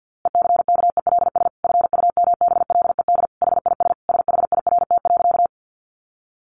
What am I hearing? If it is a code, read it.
Morse code, heard as a keyed tone, about 50 wpm, 700 Hz. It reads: ECKELS FUMBLED HIS SHIRT.
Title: A Sound Of Thunder 50wpm